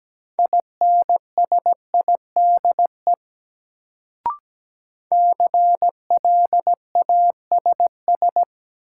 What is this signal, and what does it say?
Morse code, heard as a keyed tone, about 17 wpm, 700 Hz. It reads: INSIDE CLASS
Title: Sentences from Top 1000 Words - Review Part 2 - New Word Drill - 17wpm